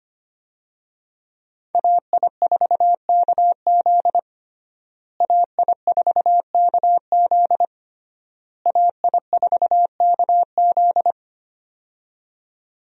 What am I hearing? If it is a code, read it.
Morse code, heard as a keyed tone, about 25 wpm, 700 Hz. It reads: AI4KZ AI4KZ AI4KZ
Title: USA Call Signs - 5 Characters - Repeated 3x - 25wpm